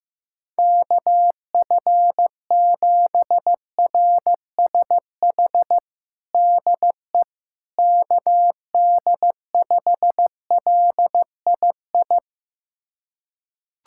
Morse code, heard as a keyed tone, about 15 wpm, 700 Hz. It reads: KF7RSH DE KD5LII
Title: USA Callsign DE Callsign 15wpm